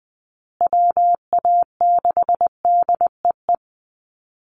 Morse code, heard as a keyed tone, about 20 wpm, 700 Hz. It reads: WA6DEE
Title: USA Call Signs Repeated 2x 20wpm